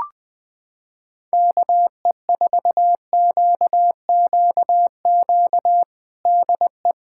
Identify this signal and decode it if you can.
Morse code, heard as a keyed tone, about 20 wpm, 700 Hz. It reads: KE4QQQ DE
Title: USA Callsign DE Callsign - 6 Characters - 20wpm